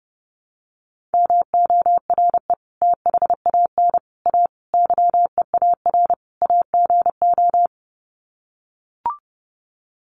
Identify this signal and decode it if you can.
Morse code, heard as a keyed tone, about 30 wpm, 700 Hz. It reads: MORE THAN A YEAR AGO
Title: Sets of 5 Words Encore2 30wpm